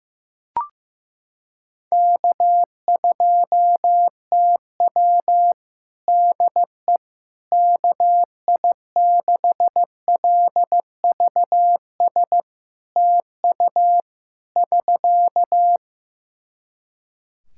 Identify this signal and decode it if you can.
Morse code, heard as a keyed tone, about 15 wpm, 700 Hz. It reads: K2TW DE KI6LVS TU <SK>
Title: Practice QSO - Exchange 4 - Ending - 15wpm